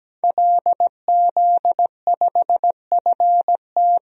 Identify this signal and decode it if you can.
Morse code, heard as a keyed tone, about 17 wpm, 700 Hz. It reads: LZ5FT
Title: DXCC-100 Call Signs - 5 Characters - Repeated 2x - 17wpm